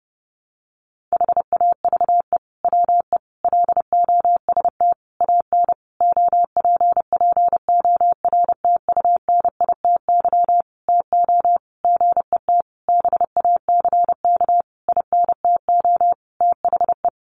Morse code, heard as a keyed tone, about 28 wpm, 700 Hz. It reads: HAVE WE LOST AN OPPORTUNITY TO GET BACK INTO THE
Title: Sentences from Top 1400 Words 30wpm